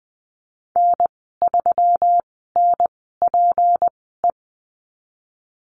Morse code, heard as a keyed tone, about 20 wpm, spaced as for 15 wpm, 700 Hz. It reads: N3NPE
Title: 179 Lesson 29g - Slash - Call Signs 20wpm